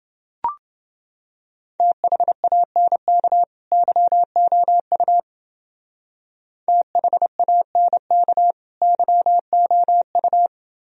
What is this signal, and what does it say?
Morse code, heard as a keyed tone, about 28 wpm, 700 Hz. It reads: THANK YOU THANK YOU